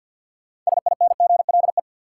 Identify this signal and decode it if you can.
Morse code, heard as a keyed tone, about 49 wpm, 700 Hz. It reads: SINGLE
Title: Six Letter Words - 50wpm